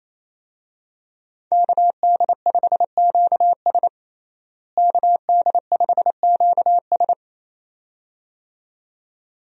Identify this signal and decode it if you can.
Morse code, heard as a keyed tone, about 28 wpm, 700 Hz. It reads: KD5QS KD5QS
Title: USA Call Signs - 5 Characters - Repeated 2x - 28wpm